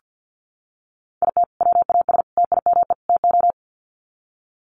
Morse code, heard as a keyed tone, about 48 wpm, 700 Hz. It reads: IT WAS TIME TO
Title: Sets of 4 Words 50wpm